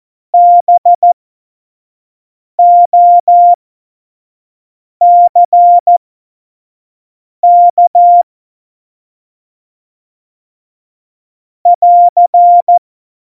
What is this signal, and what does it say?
Morse code, heard as a keyed tone, about 14 wpm, spaced as for 5 wpm, 700 Hz. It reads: BOCK <AR>
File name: Morsekurs / Witze / Witz 10.Juli 2018, 25 F70